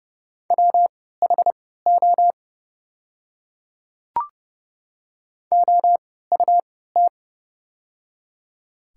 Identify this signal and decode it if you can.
Morse code, heard as a keyed tone, about 30 wpm, spaced as for 17 wpm, 690 Hz. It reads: WHO OUT
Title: 3x Character Spacing - Top 300 Words - 30wpm